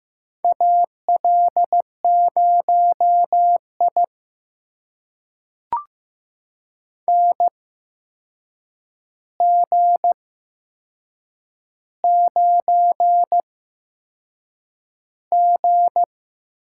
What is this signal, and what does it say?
Morse code, heard as a keyed tone, about 15 wpm, spaced as for 6 wpm, 700 Hz. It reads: AL0I NG9G
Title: USA Call Signs - 4 Characters - 8x Character Spacing - 15wpm